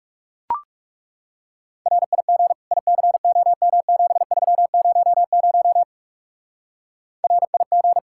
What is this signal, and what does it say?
Morse code, heard as a keyed tone, about 36 wpm, 700 Hz. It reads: RIG ICOM7300 RIG
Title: Speed-Racing Common QSO Phrases - 30wpm